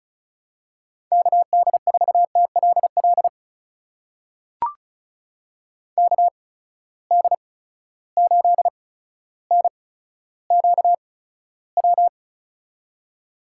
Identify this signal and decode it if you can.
Morse code, heard as a keyed tone, about 35 wpm, spaced as for 15 wpm, 700 Hz. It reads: KD4TLL KD8NQW
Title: USA Call Signs - 6 Characters - 8x Character Spacing - 35wpm